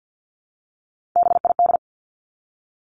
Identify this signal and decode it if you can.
Morse code, heard as a keyed tone, about 50 wpm, 700 Hz. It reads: BID
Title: Three Challenging Letter-Number Combinations 50wpm